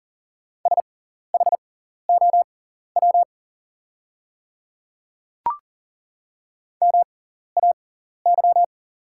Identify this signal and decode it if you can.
Morse code, heard as a keyed tone, about 40 wpm, spaced as for 14 wpm, 700 Hz. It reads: SHOW MAY
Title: 6x Character Spacing - Top 500 Words - 40wpm